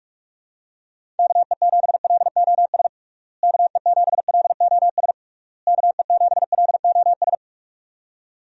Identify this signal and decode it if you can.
Morse code, heard as a keyed tone, about 41 wpm, 700 Hz. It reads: KE7LOS KE7LOS KE7LOS
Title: USA Call Signs - 6 Characters - Repeated 3x - 45wpm